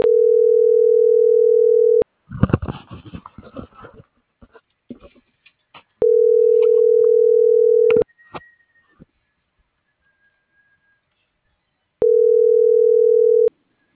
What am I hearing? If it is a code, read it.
Ringback tone (North America), 440 + 480 Hz.